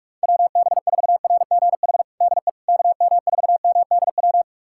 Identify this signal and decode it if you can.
Morse code, heard as a keyed tone, about 43 wpm, 700 Hz. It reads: WB4RGH DE KM4MDW
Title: USA Callsign DE Callsign 45wpm